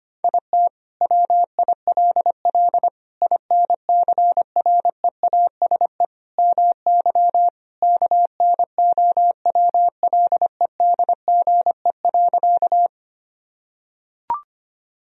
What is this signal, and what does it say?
Morse code, heard as a keyed tone, about 25 wpm, 700 Hz. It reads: IT WILL INCREASE MY KNOWLEDGE.
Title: Sentences from Top 1000 Words - Review Part 4 - 25wpm